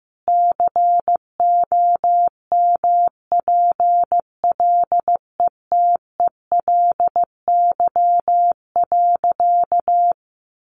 Morse code, heard as a keyed tone, about 15 wpm, 700 Hz. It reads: COMPLETELY.
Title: Sentences from Top 1400 Words 15wpm